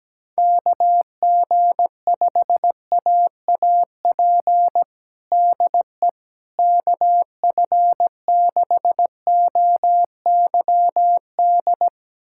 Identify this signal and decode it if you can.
Morse code, heard as a keyed tone, about 17 wpm, 700 Hz. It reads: KG5AAP DE KF6OYD
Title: USA Callsign DE Callsign - 6 Characters - 17wpm